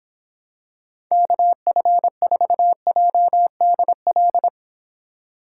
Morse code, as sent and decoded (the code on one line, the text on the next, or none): -.- ..-. ....- .--- -.. .-..
KF4JDL